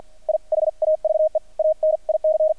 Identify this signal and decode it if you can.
Morse code, heard as a keyed tone, about 50 wpm, spaced as for 44 wpm, 620 Hz. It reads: I HAVE AN IC
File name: archived / CRSnet